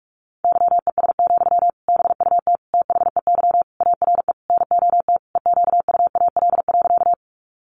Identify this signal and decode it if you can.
Morse code, heard as a keyed tone, about 44 wpm, 700 Hz. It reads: YES, BUT THEY ARE NOT EQUAL.
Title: Sentences from Top 1400 Words 45wpm